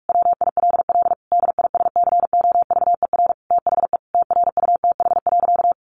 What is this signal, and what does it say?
Morse code, heard as a keyed tone, about 46 wpm, 700 Hz. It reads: WILL DISCOVER THE TRUTH.
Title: Sentences from Top 1100 Words 45wpm